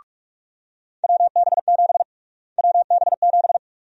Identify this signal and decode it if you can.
Morse code, heard as a keyed tone, about 42 wpm, 700 Hz. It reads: WB7 WB7
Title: USA Call Sign Prefixes - Repeated 2x - 45wpm